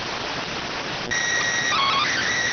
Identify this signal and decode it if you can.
SSTV (slow-scan television), PD 290.
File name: findings / CT AXIAL -572mm 2.gen-38 rx